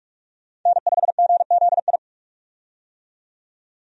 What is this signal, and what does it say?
Morse code, heard as a keyed tone, about 46 wpm, 700 Hz. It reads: N5GZI
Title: USA Call Signs - 5 Characters - 45wpm